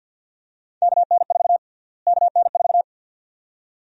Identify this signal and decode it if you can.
Morse code, heard as a keyed tone, about 50 wpm, 700 Hz. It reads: KN4 KN4
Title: USA Call Sign Prefixes - Repeated 2x - 50wpm